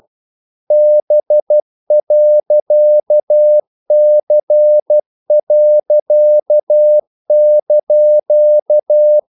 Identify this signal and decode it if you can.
Morse code, heard as a keyed tone, about 12 wpm, 600 Hz. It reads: B.C.)
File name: CW / Practice / EnglishQuotations0112